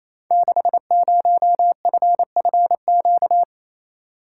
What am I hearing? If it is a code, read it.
Morse code, heard as a keyed tone, about 29 wpm, 700 Hz. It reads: 60FFQ
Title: Code Groups - 5 Letter:Numbers - 28wpm